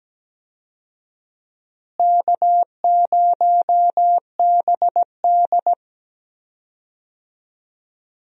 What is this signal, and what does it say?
Morse code, heard as a keyed tone, about 17 wpm, 700 Hz. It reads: K0BD
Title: USA Call Signs - 4 Characters - Repeated 2x - 17wpm